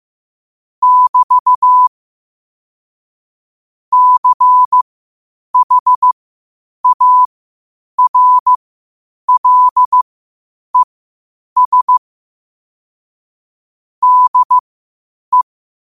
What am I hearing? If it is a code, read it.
Morse code, heard as a keyed tone, about 15 wpm, spaced as for 8 wpm, 1000 Hz. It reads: <BT> CHARLES DE
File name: qotdfiles / 8wpm